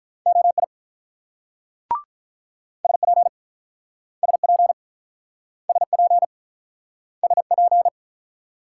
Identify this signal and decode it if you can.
Morse code, heard as a keyed tone, about 48 wpm, 700 Hz. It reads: OI SP SP SP SP